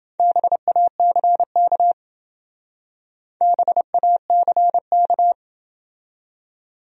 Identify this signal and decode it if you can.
Morse code, heard as a keyed tone, about 28 wpm, 700 Hz. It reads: BACK BACK